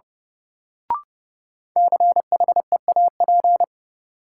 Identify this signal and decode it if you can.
Morse code, heard as a keyed tone, about 32 wpm, 700 Hz. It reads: CHEAP
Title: Sentences from Top 1200 Words - New Word Drill Alt - 30wpm